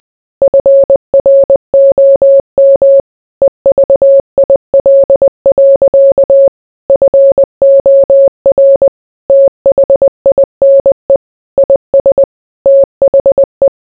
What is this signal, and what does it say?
Morse code, heard as a keyed tone, about 20 wpm, 550 Hz. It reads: FROM EVIL. FOR THINE IS THE